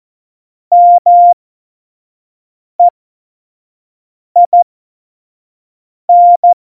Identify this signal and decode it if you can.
Morse code, heard as a keyed tone, about 14 wpm, spaced as for 5 wpm, 700 Hz. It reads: MEIN
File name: Morsekurs / Witze / Witz 29. Dezember 2020, 25 F70